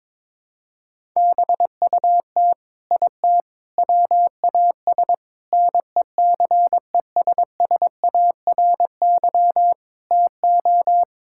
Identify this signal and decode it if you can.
Morse code, heard as a keyed tone, about 21 wpm, 700 Hz. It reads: BUT IT WAS NECESSARY TO
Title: Sentences from Top 1000 Words - Review Part 6 - 22wpm